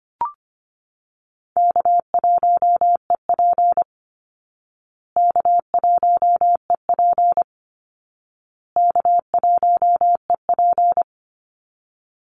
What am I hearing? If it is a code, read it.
Morse code, heard as a keyed tone, about 25 wpm, 700 Hz. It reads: K1EP K1EP K1EP
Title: Easy Call Signs Repeated 3x - 25wpm